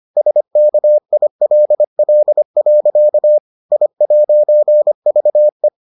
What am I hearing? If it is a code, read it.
Morse code, heard as a keyed tone, about 25 wpm, 600 Hz. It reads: SKILL. I'VE